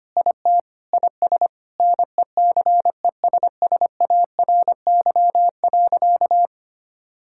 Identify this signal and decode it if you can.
Morse code, heard as a keyed tone, about 25 wpm, 700 Hz. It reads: IT IS NECESSARY.
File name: practice / 25wpm / Sentences with 9-Word Phrases - 25wpm